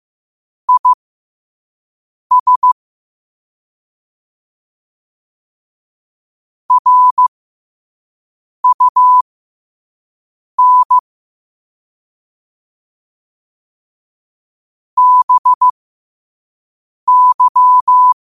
Morse code, heard as a keyed tone, about 15 wpm, spaced as for 5 wpm, 1000 Hz. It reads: IS RUN BY
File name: qotdfiles / 5wpm